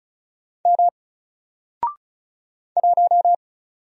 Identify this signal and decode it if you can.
Morse code, heard as a keyed tone, about 36 wpm, 700 Hz. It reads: M 1
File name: Single-Letter-Number-35wpm